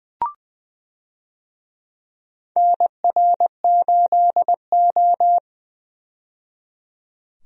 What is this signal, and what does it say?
Morse code, heard as a keyed tone, about 20 wpm, 700 Hz. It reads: NR8O
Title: USA Call Signs - 4 Characters - 20wpm